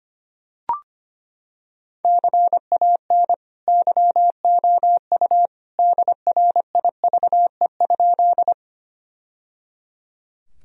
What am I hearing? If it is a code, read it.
Morse code, heard as a keyed tone, about 25 wpm, 700 Hz. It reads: CAN YOU DRIVE?
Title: Sentences from Top 1000 Words - Review Part 2 - 25wpm